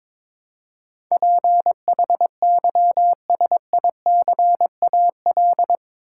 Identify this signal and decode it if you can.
Morse code, heard as a keyed tone, about 22 wpm, 700 Hz. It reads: PHYSICAL
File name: practice / 22wpm / Eight Letter Words - 22wpm